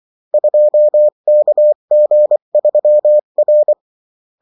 Morse code, heard as a keyed tone, about 24 wpm, 600 Hz. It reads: 2KG3R
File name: morse mp3 / revision9-vitesse 22 mots